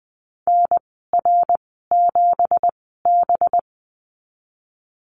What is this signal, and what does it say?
Morse code, heard as a keyed tone, about 20 wpm, spaced as for 14 wpm, 700 Hz. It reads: NR7B
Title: 186 Lesson 30g - Letter b - Call Signs 20wpm